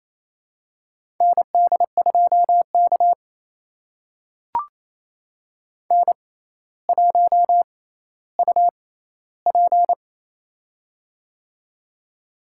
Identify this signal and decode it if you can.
Morse code, heard as a keyed tone, about 28 wpm, spaced as for 14 wpm, 700 Hz. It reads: ND2K N1UP
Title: USA Call Signs - 4 Characters - 6x Character Spacing - 28wpm